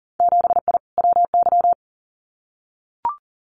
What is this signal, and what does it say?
Morse code, heard as a keyed tone, about 40 wpm, 700 Hz. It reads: 7I WY
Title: Winter Field Day Exchanges Encore 40wpm